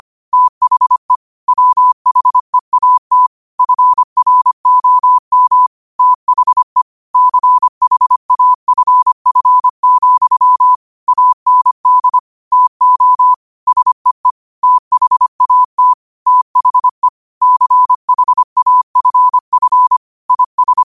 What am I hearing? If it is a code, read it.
Morse code, heard as a keyed tone, about 25 wpm, 1000 Hz. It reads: THE WHEAT FROM THE CHAFF, AND TO SEE THAT THE CHAFF IS
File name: qotdfiles / 25wpm